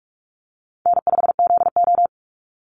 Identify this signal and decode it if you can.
Morse code, heard as a keyed tone, about 44 wpm, 700 Hz. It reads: N5ZO
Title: Call Signs - 45wpm